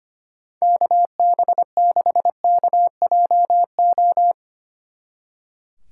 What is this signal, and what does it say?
Morse code, heard as a keyed tone, about 32 wpm, 700 Hz. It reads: KB6KJO